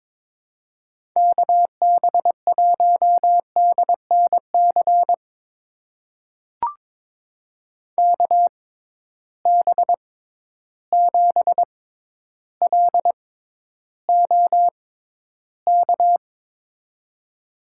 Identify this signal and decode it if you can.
Morse code, heard as a keyed tone, about 22 wpm, spaced as for 11 wpm, 700 Hz. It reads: KB1DNC KB7LOK
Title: USA Call Signs - 6 Characters - 6x Character Spacing - 22wpm